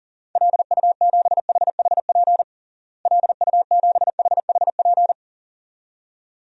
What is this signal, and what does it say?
Morse code, heard as a keyed tone, about 40 wpm, 700 Hz. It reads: LU7HHP LU7HHP
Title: DXCC-100 Call Signs - 6 Characters - Repeated 2x - 40wpm